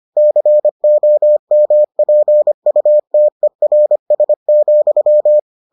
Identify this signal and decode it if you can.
Morse code, heard as a keyed tone, about 25 wpm, 600 Hz. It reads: COMPUTERS,